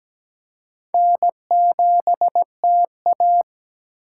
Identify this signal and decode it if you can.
Morse code, heard as a keyed tone, about 17 wpm, 700 Hz. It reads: N7TA
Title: USA Call Signs - 4 Characters - 6x Character Spacing - 17wpm